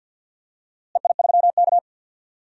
Morse code, heard as a keyed tone, about 52 wpm, 700 Hz. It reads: EI3K
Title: DXCC-100 Call Signs - 4 Characters - Repeated 2x - 50wpm